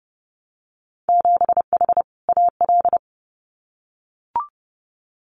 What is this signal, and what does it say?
Morse code, heard as a keyed tone, about 30 wpm, 700 Hz. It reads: 7H AL
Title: Winter Field Day Exchanges Encore 30wpm